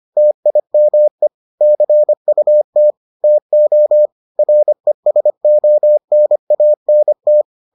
Morse code, heard as a keyed tone, about 25 wpm, 600 Hz. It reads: TIME CUT TO RESONANT